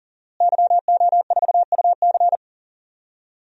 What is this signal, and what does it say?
Morse code, heard as a keyed tone, about 39 wpm, 700 Hz. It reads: YO4UC
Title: DXCC-100 Call Signs - 5 Characters - Repeated 2x - 40wpm